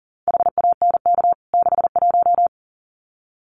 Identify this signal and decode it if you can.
Morse code, heard as a keyed tone, about 40 wpm, 700 Hz. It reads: HANK 61
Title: CWT Exchange 40wpm